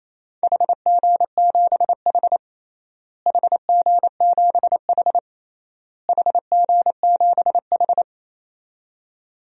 Morse code, heard as a keyed tone, about 28 wpm, 700 Hz. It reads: HG7H HG7H HG7H
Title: DXCC-100 Call Signs - 4 Characters - Repeated 3x - 28wpm